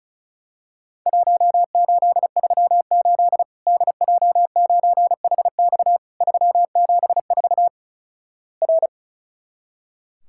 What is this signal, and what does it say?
Morse code, heard as a keyed tone, about 36 wpm, 700 Hz. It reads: 1838 DJ9HX 374 R
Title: Worked All Europe DX Contest - QTC Exchange - 35wpm